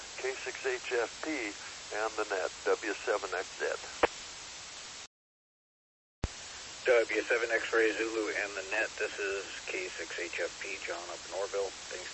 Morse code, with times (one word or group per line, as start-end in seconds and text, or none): none